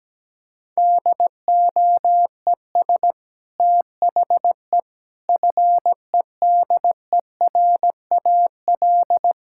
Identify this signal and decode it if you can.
Morse code, heard as a keyed tone, about 17 wpm, 700 Hz. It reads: DOES THE FEDERAL
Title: Sentences from Top 1000 Words - Review Part 5 - 17wpm